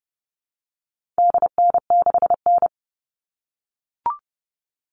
Morse code, heard as a keyed tone, about 30 wpm, 700 Hz. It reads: DN6N
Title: All Cut Numbers - 4-Digit 30wpm